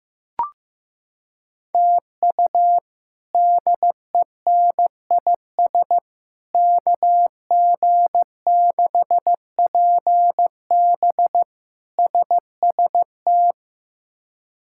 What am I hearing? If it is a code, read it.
Morse code, heard as a keyed tone, about 15 wpm, 700 Hz. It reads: TU DENIS KG6PB SST
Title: K1USN SST QSO Practice - 15wpm